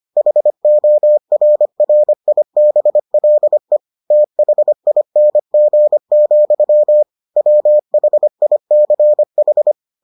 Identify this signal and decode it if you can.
Morse code, heard as a keyed tone, about 25 wpm, 600 Hz. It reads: HORRIBLE THING, WHICH